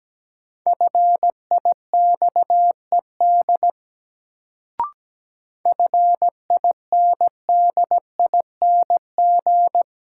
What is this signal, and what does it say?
Morse code, heard as a keyed tone, about 17 wpm, 700 Hz. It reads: FIXED FINDING
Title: Sentences from Top 1100 Words - New Word Drill - 17wpm